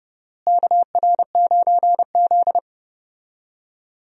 Morse code, heard as a keyed tone, about 30 wpm, 700 Hz. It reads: KR9Z
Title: USA Call Signs - 4 Characters - Repeated 3x - 30wpm